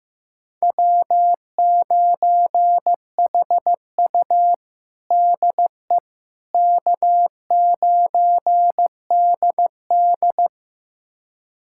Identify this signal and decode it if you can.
Morse code, heard as a keyed tone, about 15 wpm, 700 Hz. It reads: W9HU DE K9DD
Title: USA Callsign DE Callsign - 4 Characters - 15wpm